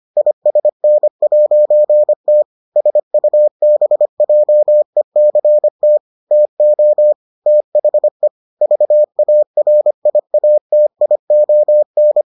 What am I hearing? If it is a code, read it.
Morse code, heard as a keyed tone, about 25 wpm, 600 Hz. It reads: ISN'T SUBJECT TO THE VARIATION